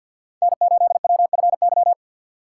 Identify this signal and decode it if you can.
Morse code, heard as a keyed tone, about 52 wpm, 700 Hz. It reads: N8WFY
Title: USA Call Signs - 5 Characters - 3x Character Spacing - 50wpm